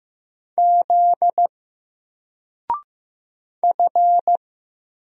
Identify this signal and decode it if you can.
Morse code, heard as a keyed tone, about 15 wpm, 700 Hz. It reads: Z F
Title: Single Letter-Number 15wpm